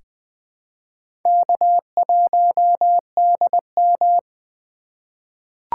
Morse code, heard as a keyed tone, about 20 wpm, 700 Hz. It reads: K1DM
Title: K1USN SST QSO Practice - 20wpm